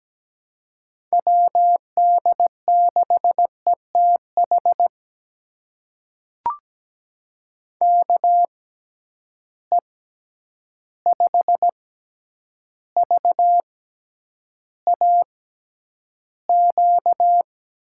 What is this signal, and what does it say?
Morse code, heard as a keyed tone, about 17 wpm, spaced as for 9 wpm, 700 Hz. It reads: WD6ETH KE5VAQ
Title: USA Call Signs - 6 Characters - 6x Character Spacing - 17wpm